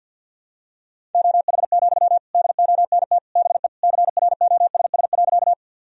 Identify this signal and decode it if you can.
Morse code, heard as a keyed tone, about 52 wpm, 700 Hz. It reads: OH, DONT BE CROSS.
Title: Sentences from Top 1000 Words - Review Part 1 - 50wpm